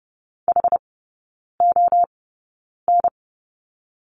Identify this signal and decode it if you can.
Morse code, heard as a keyed tone, about 30 wpm, spaced as for 9 wpm, 700 Hz. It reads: HON
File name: 081 Lesson 14e - Number 2 - Three Characters 30wpm